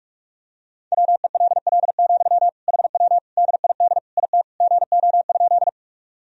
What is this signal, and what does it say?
Morse code, heard as a keyed tone, about 46 wpm, 700 Hz. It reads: WELL, HW DID IT GO?
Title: Sentences from Top 200 Words and Abbreviations 45wpm